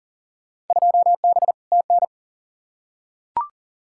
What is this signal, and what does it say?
Morse code, heard as a keyed tone, about 40 wpm, 700 Hz. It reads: 2B TN
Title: Field Day Exchanges Encore - 40wpm